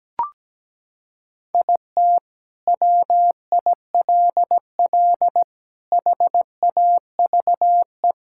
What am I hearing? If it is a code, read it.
Morse code, heard as a keyed tone, about 17 wpm, 700 Hz. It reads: IT WILL HAVE
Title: Sentences from Top 1000 Words - Review Part 6 - 17wpm